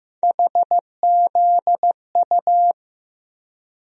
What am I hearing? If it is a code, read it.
Morse code, heard as a keyed tone, about 15 wpm, 700 Hz. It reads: HZU
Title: USA Call Sign Postfixes - 3 Letters 15wpm